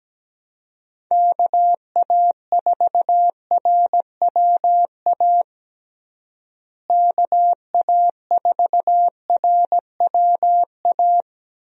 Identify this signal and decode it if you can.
Morse code, heard as a keyed tone, about 17 wpm, 700 Hz. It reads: KA4RWA KA4RWA
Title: USA Call Signs - 6 Characters - Repeated 2x - 17wpm